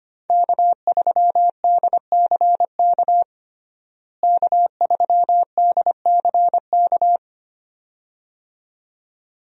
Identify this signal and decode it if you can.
Morse code, heard as a keyed tone, about 25 wpm, 700 Hz. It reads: K3DCK K3DCK
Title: USA Call Signs - 5 Characters - Repeated 2x - 25wpm